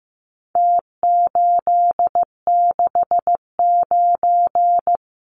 Morse code, heard as a keyed tone, about 15 wpm, 700 Hz. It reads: T869
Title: Numbers - Contest Style - 4-Digit 15wpm